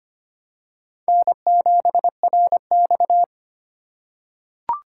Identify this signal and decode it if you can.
Morse code, heard as a keyed tone, about 25 wpm, 700 Hz. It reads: N7RX
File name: practice / 25wpm / Four Letter-Number Combinations - 25wpm